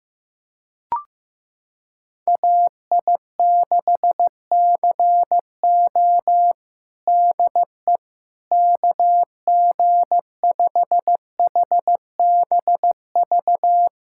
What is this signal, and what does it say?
Morse code, heard as a keyed tone, about 15 wpm, 700 Hz. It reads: AI6CO DE KG5HBV
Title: USA Callsign DE Callsign 15wpm